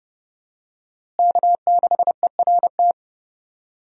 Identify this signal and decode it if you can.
Morse code, heard as a keyed tone, about 30 wpm, 700 Hz. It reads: K6ERT
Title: USA Call Signs - 5 Characters - Repeated 3x - 30wpm